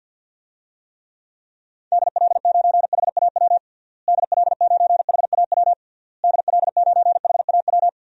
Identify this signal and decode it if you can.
Morse code, heard as a keyed tone, about 53 wpm, 700 Hz. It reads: DL9HAW DL9HAW DL9HAW
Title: DXCC-100 Call Signs - 6 Characters - Repeated 3x - 50wpm